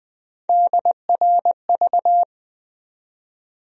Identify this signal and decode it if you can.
Morse code, heard as a keyed tone, about 20 wpm, 700 Hz. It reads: DRV
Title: USA Call Sign Postfixes - 3 Letters 20wpm